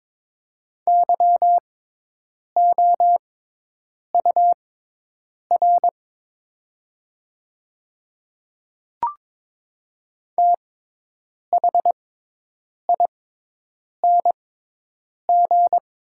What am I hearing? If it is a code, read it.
Morse code, heard as a keyed tone, about 22 wpm, spaced as for 8 wpm, 700 Hz. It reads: YOUR THING
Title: 6x Character Spacing - Top 300 Words - 22wpm